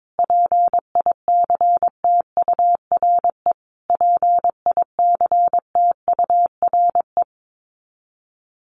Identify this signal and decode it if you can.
Morse code, heard as a keyed tone, about 22 wpm, 700 Hz. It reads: PICTURE PICTURE
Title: Top 200 Words Repeated 22wpm